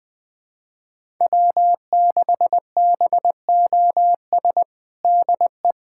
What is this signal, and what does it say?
Morse code, heard as a keyed tone, about 20 wpm, 700 Hz. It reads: W6BOS DE
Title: USA Callsign DE Callsign - 5 Characters - 20wpm